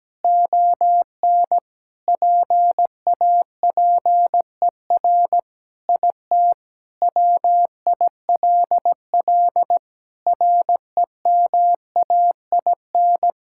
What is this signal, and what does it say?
Morse code, heard as a keyed tone, about 17 wpm, 700 Hz. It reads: ON PAPER IT WILL REMAIN
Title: Sentences from Top 1600 Words 17wpm